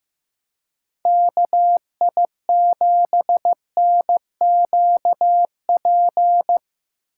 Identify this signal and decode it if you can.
Morse code, heard as a keyed tone, about 17 wpm, 700 Hz. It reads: KI7NQP
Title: Speed Racing - USA 2x3 Call Signs 15wpm